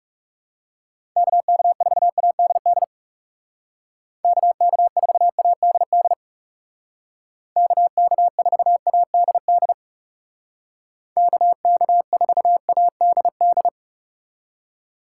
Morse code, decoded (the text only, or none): KK4ADD KK4ADD KK4ADD KK4ADD